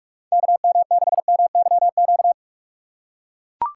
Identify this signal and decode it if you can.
Morse code, heard as a keyed tone, about 44 wpm, 700 Hz. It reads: KM6MYQ